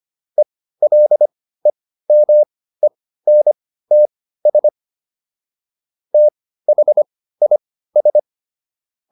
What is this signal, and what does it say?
Morse code, heard as a keyed tone, about 25 wpm, spaced as for 15 wpm, 600 Hz. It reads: ELEMENTS THIS